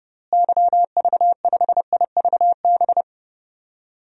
Morse code, heard as a keyed tone, about 30 wpm, 700 Hz. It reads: YV5IVB
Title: DXCC-100 Call Signs - 6 Characters - Repeated 3x - 30wpm